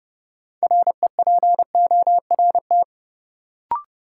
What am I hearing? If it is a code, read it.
Morse code, heard as a keyed tone, about 32 wpm, 700 Hz. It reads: REPORT
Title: Sentences from Top 900 Words - New Word Drill - 30wpm